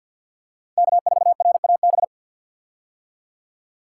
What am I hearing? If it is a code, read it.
Morse code, heard as a keyed tone, about 47 wpm, 700 Hz. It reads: K4RAB
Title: K1USN SST QSO Practice - 50wpm